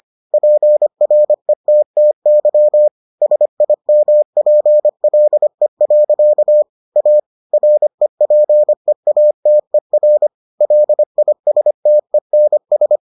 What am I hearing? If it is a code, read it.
Morse code, heard as a keyed tone, about 25 wpm, 600 Hz. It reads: PRETTY SIMPLE. A REPEATER LISTENS